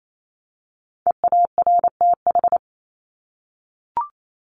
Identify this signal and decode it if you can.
Morse code, heard as a keyed tone, about 28 wpm, 700 Hz. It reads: EARTH
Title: Top 300 Words Repeated 28wpm